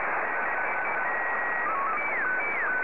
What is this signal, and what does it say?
SSTV (slow-scan television), Scottie 1.